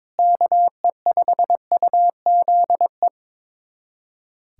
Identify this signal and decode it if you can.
Morse code, heard as a keyed tone, about 21 wpm, 700 Hz. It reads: KE5UZE
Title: USA Callsign DE Callsign - 6 Characters - 22wpm